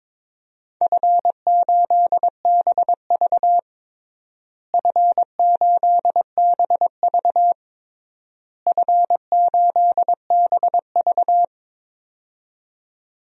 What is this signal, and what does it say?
Morse code, heard as a keyed tone, about 22 wpm, 700 Hz. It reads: F8BV F8BV F8BV
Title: DXCC-100 Call Signs - 4 Characters - Repeated 3x - 22wpm